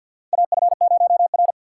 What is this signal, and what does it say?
Morse code, heard as a keyed tone, about 52 wpm, 700 Hz. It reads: AF0R
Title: USA Call Signs - 4 Characters - 2x Character Spacing - 50wpm